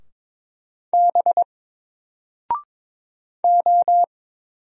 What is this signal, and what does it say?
Morse code, heard as a keyed tone, about 22 wpm, 700 Hz. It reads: B O
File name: Single-Letter-Number-22wpm